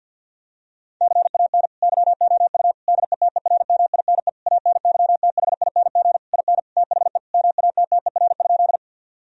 Morse code, heard as a keyed tone, about 50 wpm, 700 Hz. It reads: CAN YOU DETERMINE ANYTHING IN THE MATTER?
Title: Sentences from Top 1000 Words - Review Part 4 - 50wpm